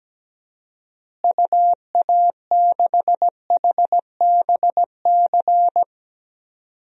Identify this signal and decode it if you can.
Morse code, heard as a keyed tone, about 17 wpm, 700 Hz. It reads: UA6HBC
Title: DXCC-100 Call Signs - 6 Characters - Repeated 2x - 17wpm